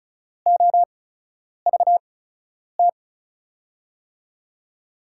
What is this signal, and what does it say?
Morse code, heard as a keyed tone, about 36 wpm, spaced as for 10 wpm, 700 Hz. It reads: OVT
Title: USA Call Signs - 5 Characters - 8x Character Spacing - 35wpm